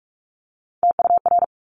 Morse code, heard as a keyed tone, about 46 wpm, 700 Hz. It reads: TUR
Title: Top Three Letter Combinations in Words 45wpm